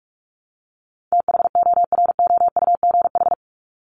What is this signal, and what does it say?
Morse code, heard as a keyed tone, about 44 wpm, 700 Hz. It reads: THOROUGH
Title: LY Suffix - Top 100 Words with Base Words 45wpm